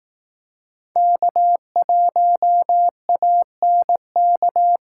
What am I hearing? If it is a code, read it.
Morse code, heard as a keyed tone, about 18 wpm, 700 Hz. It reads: K1ANK
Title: Speed Racing - USA 1x3 Call Signs 15wpm